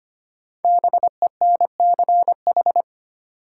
Morse code, heard as a keyed tone, about 25 wpm, 700 Hz. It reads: BENCH
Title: Sentences from Top 1400 Words - New Word Drill Alt - 25wpm